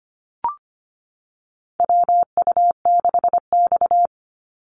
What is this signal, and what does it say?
Morse code, heard as a keyed tone, about 25 wpm, 700 Hz. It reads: WU6X
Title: Call Signs - 25wpm